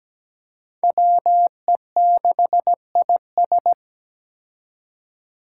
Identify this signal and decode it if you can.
Morse code, heard as a keyed tone, about 17 wpm, 700 Hz. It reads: WE6IS
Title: Five Challenging Letter-Number Combinations 17wpm